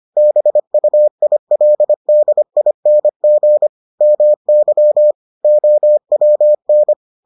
Morse code, heard as a keyed tone, about 25 wpm, 600 Hz. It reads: BUILDING MY OWN